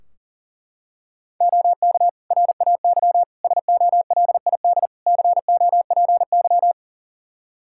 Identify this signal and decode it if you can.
Morse code, heard as a keyed tone, about 40 wpm, 700 Hz. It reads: OK RAY SOLID COPY
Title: Practice QSO - Exchange 3 - More Info - 40wpm